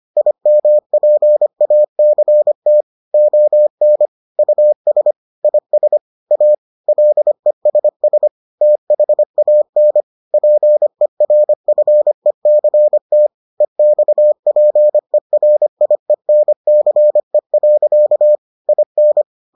Morse code, heard as a keyed tone, about 25 wpm, 600 Hz. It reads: IMPACT ON US IS A LESS THAN PERFECT EXPERIENCE. IN